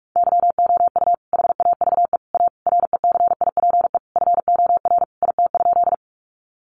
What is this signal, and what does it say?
Morse code, heard as a keyed tone, about 44 wpm, 700 Hz. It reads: YOU HAVE A RECIPE FOR IT?
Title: Sentences from Top 1200 Words 45wpm